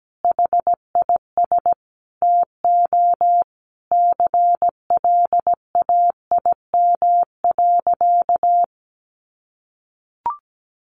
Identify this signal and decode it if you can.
Morse code, heard as a keyed tone, about 17 wpm, 700 Hz. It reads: HIS TO CLAIM.
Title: Sentences from Top 1500 Words 17wpm